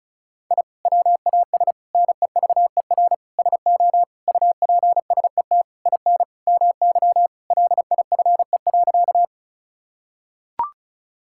Morse code, heard as a keyed tone, about 34 wpm, 700 Hz. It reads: I WAS NEVER SO UPSET IN MY LIFE.
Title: Sentences from Top 1000 Words - Review Part 5 - 35wpm